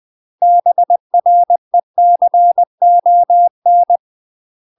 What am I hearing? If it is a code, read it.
Morse code, heard as a keyed tone, about 20 wpm, 700 Hz. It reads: BRECON
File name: UK MASTER QSOs TRAINING / 20WPM / UK MASTER QSOs 0074